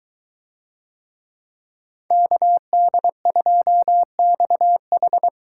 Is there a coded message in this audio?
Yes: Morse code from 2.1 to 5.3 s, heard as a keyed tone, about 22 wpm, 700 Hz. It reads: KD2XH